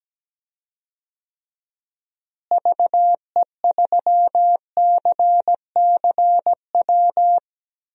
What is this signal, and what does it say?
Morse code, heard as a keyed tone, about 17 wpm, 700 Hz. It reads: VE3CCW
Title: DXCC-100 Call Signs - 6 Characters - Repeated 3x - 17wpm